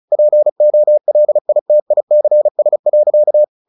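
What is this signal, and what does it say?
Morse code, heard as a keyed tone, about 34 wpm, 600 Hz. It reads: POLITICS.
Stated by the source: Morse practice at 35 wpm from CW med Gnister, sending Ken Follett's novel Never in English.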